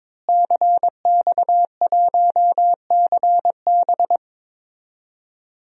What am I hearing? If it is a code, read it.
Morse code, heard as a keyed tone, about 21 wpm, 700 Hz. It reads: CX1CB
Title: DXCC-100 Call Signs - 5 Characters - Repeated 3x - 22wpm